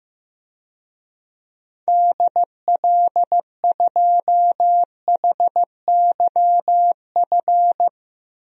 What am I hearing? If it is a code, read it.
Morse code, heard as a keyed tone, about 15 wpm, 700 Hz. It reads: DL2HYF
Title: DXCC-100 Call Signs - 6 Characters - Repeated 2x - 15wpm